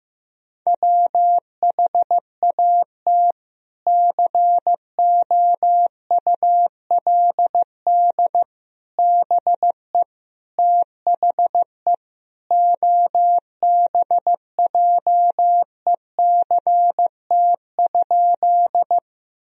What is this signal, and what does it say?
Morse code, heard as a keyed tone, about 15 wpm, 700 Hz. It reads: WHAT COULD BE THE OBJECT?
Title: Sentences from Top 1000 Words - Review Part 3 - 15wpm